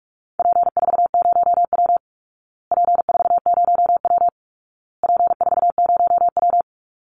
Morse code, heard as a keyed tone, about 44 wpm, 700 Hz. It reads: P40W P40W P40W
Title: Call Signs Repeated 2x - 45wpm